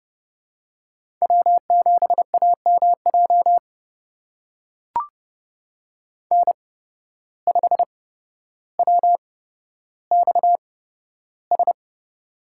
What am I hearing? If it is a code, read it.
Morse code, heard as a keyed tone, about 30 wpm, spaced as for 13 wpm, 700 Hz. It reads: W7AMJ N5WXS
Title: USA Call Signs - 5 Characters - 8x Character Spacing - 30wpm